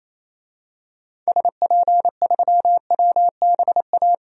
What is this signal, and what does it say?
Morse code, heard as a keyed tone, about 28 wpm, 700 Hz. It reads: SP3WBA
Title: DXCC-100 Call Signs - 6 Characters - Repeated 2x - 28wpm